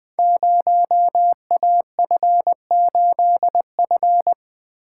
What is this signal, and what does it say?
Morse code, heard as a keyed tone, about 20 wpm, 700 Hz. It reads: 0AF8F
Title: Code Groups - 5 Letter:Numbers - 20wpm